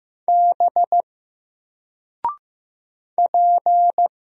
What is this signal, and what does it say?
Morse code, heard as a keyed tone, about 15 wpm, 700 Hz. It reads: B P
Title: Single Letter-Number Alt 15wpm